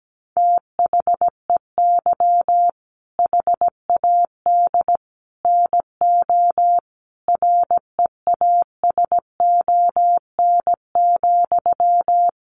Morse code, heard as a keyed tone, about 17 wpm, 700 Hz. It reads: THEY HAD NO REASON,
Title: Sentences from Top 1500 Words 17wpm